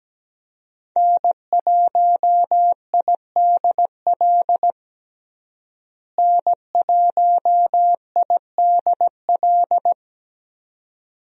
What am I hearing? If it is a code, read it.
Morse code, heard as a keyed tone, about 17 wpm, 700 Hz. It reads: N1IDL N1IDL
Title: USA Call Signs - 5 Characters - Repeated 2x - 17wpm